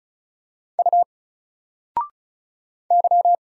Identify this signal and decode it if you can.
Morse code, heard as a keyed tone, about 33 wpm, 700 Hz. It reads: U Y
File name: Single-Letter-Number-35wpm